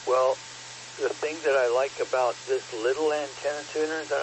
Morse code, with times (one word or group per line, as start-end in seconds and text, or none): none